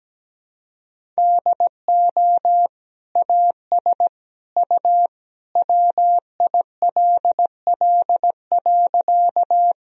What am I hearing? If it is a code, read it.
Morse code, heard as a keyed tone, about 17 wpm, 700 Hz. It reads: DO AS U WILL.
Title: Sentences from Top 100 Words and Abbreviations 17wpm